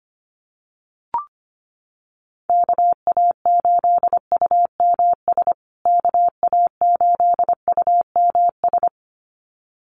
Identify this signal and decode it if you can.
Morse code, heard as a keyed tone, about 25 wpm, 700 Hz. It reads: KA8UMS KA8UMS
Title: USA Call Signs Repeated 2x 25wpm